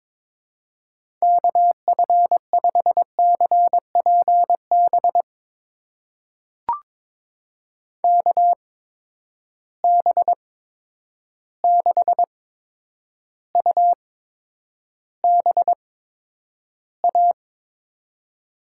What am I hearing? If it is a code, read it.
Morse code, heard as a keyed tone, about 22 wpm, spaced as for 9 wpm, 700 Hz. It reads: KF5CPB KB6UBA
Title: USA Call Signs - 6 Characters - 8x Character Spacing - 22wpm